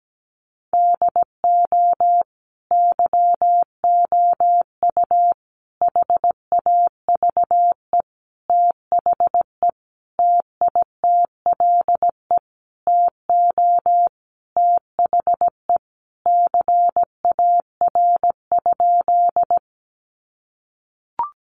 Morse code, heard as a keyed tone, about 17 wpm, 700 Hz. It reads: DO YOU HAVE THE TITLE TO THE CAR?
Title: Sentences from Top 1200 Words 17wpm